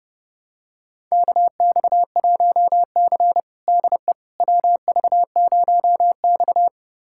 Morse code, heard as a keyed tone, about 30 wpm, 700 Hz. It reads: KX1C DE WV0X
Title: USA Callsign DE Callsign - 4 Characters - 30wpm